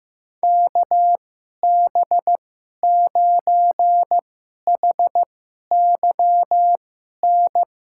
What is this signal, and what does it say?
Morse code, heard as a keyed tone, about 15 wpm, spaced as for 11 wpm, 700 Hz. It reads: KB9HYN
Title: USA Call Signs - 6 Characters - 2x Character Spacing - 15wpm